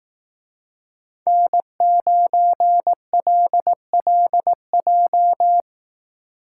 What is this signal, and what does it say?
Morse code, heard as a keyed tone, about 18 wpm, 700 Hz. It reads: N9LLJ
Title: Speed Racing - USA 1x3 Call Signs 15wpm